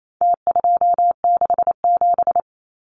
Morse code, heard as a keyed tone, about 28 wpm, 700 Hz. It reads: T267
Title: Numbers - Contest Style - 4-Digit 28wpm